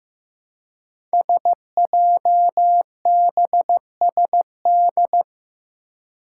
Morse code, heard as a keyed tone, about 15 wpm, 700 Hz. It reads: SJBSD
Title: Five Challenging Letter-Number Combinations 15wpm